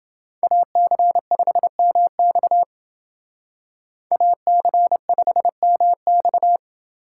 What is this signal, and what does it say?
Morse code, heard as a keyed tone, about 29 wpm, 700 Hz. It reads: AC5MX AC5MX